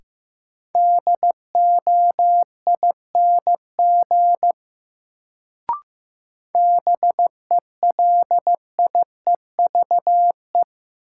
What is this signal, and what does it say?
Morse code, heard as a keyed tone, about 15 wpm, 700 Hz. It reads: DOING BELIEVE
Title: Sets of 4 Words - New Word Drill - 15wpm